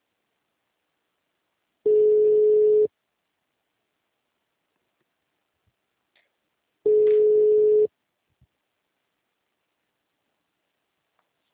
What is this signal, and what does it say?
Ringback tone (Europe, 425 Hz).